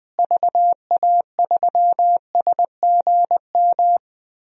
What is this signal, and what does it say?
Morse code, heard as a keyed tone, about 21 wpm, 700 Hz. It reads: VA3SGM
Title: DXCC-100 Call Signs - 6 Characters - Repeated 2x - 20wpm